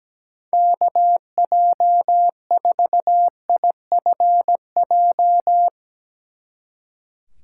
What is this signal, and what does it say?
Morse code, heard as a keyed tone, about 18 wpm, 700 Hz. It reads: KJ4IFJ
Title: Speed Racing - USA 2x3 Call Signs 17wpm